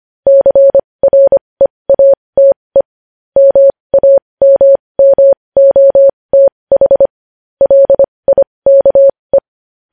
Morse code, heard as a keyed tone, about 25 wpm, spaced as for 20 wpm, 550 Hz. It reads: CREATE MAMMOTH LIKE
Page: News Headlines in Morse Code at 20 WPM